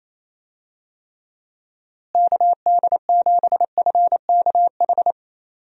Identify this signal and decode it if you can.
Morse code, heard as a keyed tone, about 28 wpm, 700 Hz. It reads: KD7FKH
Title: USA Call Signs - 6 Characters - Repeated 2x - 28wpm